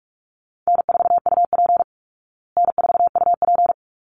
Morse code, heard as a keyed tone, about 44 wpm, 700 Hz. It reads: N4UP N4UP
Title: Easy Call Signs Repeated 2x - 45wpm